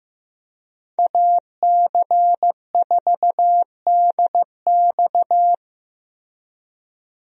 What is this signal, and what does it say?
Morse code, heard as a keyed tone, about 16 wpm, 700 Hz. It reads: AC4DX
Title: USA Call Signs - 5 Characters - Repeated 3x - 15wpm